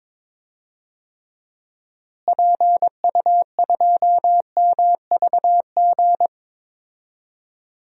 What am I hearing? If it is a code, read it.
Morse code, heard as a keyed tone, about 22 wpm, 700 Hz. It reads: PU2MVG
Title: DXCC-100 Call Signs - 6 Characters - Repeated 2x - 22wpm